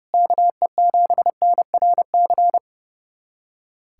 Morse code, heard as a keyed tone, about 30 wpm, 700 Hz. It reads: KE7NRC
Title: USA Callsign DE Callsign - 6 Characters - 30wpm